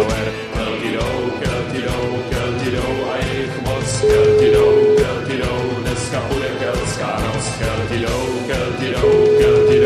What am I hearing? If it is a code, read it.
Ringback tone (Europe, 425 Hz), with speech.